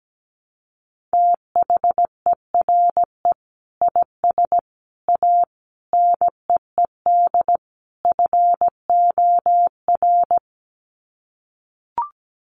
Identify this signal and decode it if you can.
Morse code, heard as a keyed tone, about 17 wpm, 700 Hz. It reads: THERE IS A NEED FOR
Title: Sets of 5 Words Encore2 17wpm